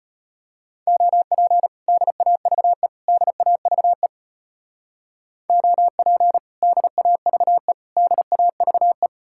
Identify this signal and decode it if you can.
Morse code, heard as a keyed tone, about 30 wpm, 700 Hz. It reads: OP DAVE DAVE OP DAVE DAVE
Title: Speed-Racing Common QSO Phrases - 25wpm